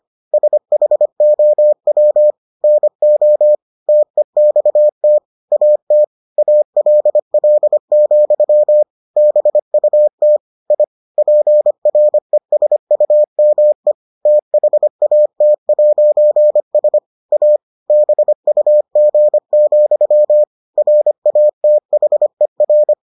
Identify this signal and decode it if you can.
Morse code, heard as a keyed tone, about 25 wpm, 600 Hz. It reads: SHOW NO TEXT AT ALL, BUT I PRESUME THAT'S A BUG, RATHER